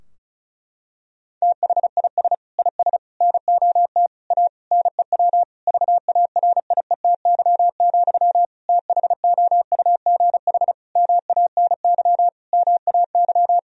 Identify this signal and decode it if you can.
Morse code, heard as a keyed tone, about 34 wpm, 700 Hz. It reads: THIS IS NOT A NEW VARIETY, THOUGH MANY MAY
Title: Sentences from Top 1000 Words - Review Part 4 - 35wpm